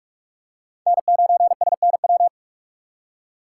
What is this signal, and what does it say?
Morse code, heard as a keyed tone, about 42 wpm, 700 Hz. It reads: N9SNW
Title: USA Call Signs - 5 Characters - 3x Character Spacing - 45wpm